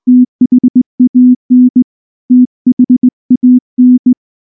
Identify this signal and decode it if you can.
Morse code, heard as a keyed tone, about 20 wpm, 250 Hz. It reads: THAN THAN